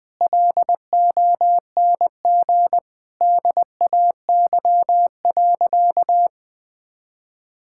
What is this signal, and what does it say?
Morse code, heard as a keyed tone, about 20 wpm, 700 Hz. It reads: LONG DAY.
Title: Sentences with 7-Word Phrases - Spoken Beginning - 20wpm